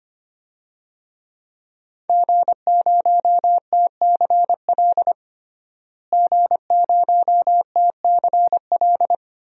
Morse code, heard as a keyed tone, about 25 wpm, 700 Hz. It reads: G0TCL G0TCL
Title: DXCC-100 Call Signs - 5 Characters - Repeated 2x - 25wpm